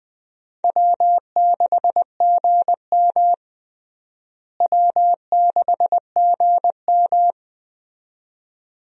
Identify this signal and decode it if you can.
Morse code, heard as a keyed tone, about 21 wpm, 700 Hz. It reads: W6GM W6GM
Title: USA Call Signs - 4 Characters - Repeated 2x - 20wpm